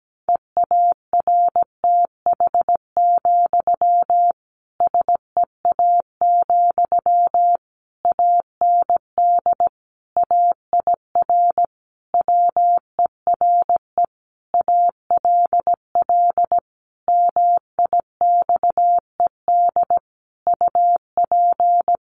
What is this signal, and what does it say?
Morse code, heard as a keyed tone, about 17 wpm, 700 Hz. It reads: EARTH, SEA, AND AIR WERE ALL MIXED UP
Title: Sentences from Top 1500 Words 17wpm